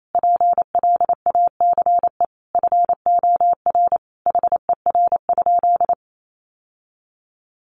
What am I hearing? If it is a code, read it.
Morse code, heard as a keyed tone, about 28 wpm, 700 Hz. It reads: PLACE FOR HER?
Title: Sentences from Top 800 Words 28wpm